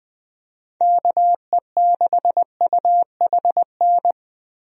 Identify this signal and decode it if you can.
Morse code, heard as a keyed tone, about 20 wpm, 700 Hz. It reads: KE6UHN
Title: USA Call Signs - 6 Characters - Repeated 3x - 20wpm